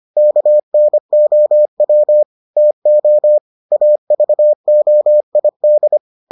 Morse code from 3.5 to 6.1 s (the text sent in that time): AVOID